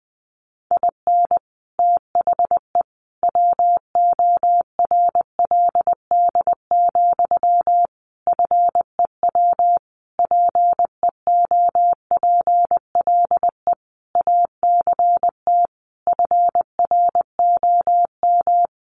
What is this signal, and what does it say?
Morse code, heard as a keyed tone, about 20 wpm, 700 Hz. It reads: IN THE WORLD, FEW PEOPLE ACT FROM
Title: Sentences from Top 1400 Words 20wpm